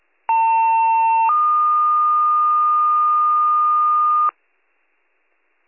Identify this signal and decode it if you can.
Two-tone page, heard as a one-second tone, then a three-second tone, about 900 then 1220 Hz, with noise. It is heard from 0.3 to 4.3 s.